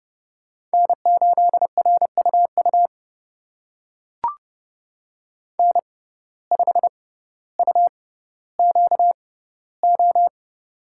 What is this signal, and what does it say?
Morse code, heard as a keyed tone, about 31 wpm, spaced as for 15 wpm, 700 Hz. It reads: N8RUU N5UQO
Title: USA Call Signs - 5 Characters - 6x Character Spacing - 30wpm